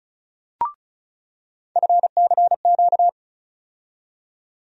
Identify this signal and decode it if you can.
Morse code, heard as a keyed tone, about 36 wpm, 700 Hz. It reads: FCQ
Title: USA Call Sign Postfixes - 3 Letters 35wpm